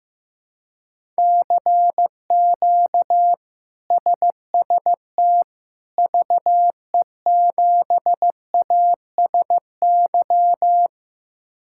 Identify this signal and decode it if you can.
Morse code, heard as a keyed tone, about 15 wpm, 700 Hz. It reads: CQ SST VE7ASY
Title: K1USN SST QSO Practice - 15wpm